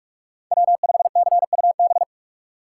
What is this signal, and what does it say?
Morse code, heard as a keyed tone, about 44 wpm, 690 Hz. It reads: W5CUB
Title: USA Call Signs - 5 Characters - 6x Character Spacing - 45wpm